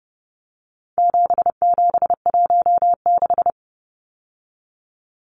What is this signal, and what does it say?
Morse code, heard as a keyed tone, about 30 wpm, 700 Hz. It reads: 7716
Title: Numbers - 4-Digit 30wpm